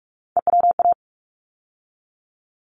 Morse code, heard as a keyed tone, about 46 wpm, 700 Hz. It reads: EWA
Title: ARRL Sections 45wpm